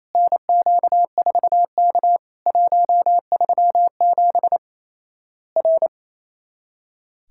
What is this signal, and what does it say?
Morse code, heard as a keyed tone, about 28 wpm, 700 Hz. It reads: NQ4K 137 R
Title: Worked All Europe DX Contest - QTC Exchange - 28wpm